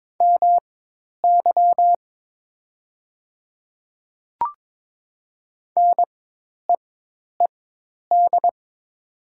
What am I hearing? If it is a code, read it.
Morse code, heard as a keyed tone, about 22 wpm, spaced as for 10 wpm, 700 Hz. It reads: MY NEED
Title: 4x Character Spacing - Top 200 Words - 22wpm